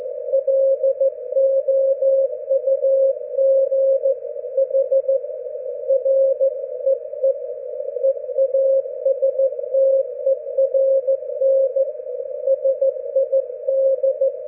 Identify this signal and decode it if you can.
Morse code, heard as a keyed tone, about 14 wpm, 540 Hz. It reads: LOUGH REE EASTERN SID
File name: GW3UEP / EI0CF P PortlickHarbour 19June12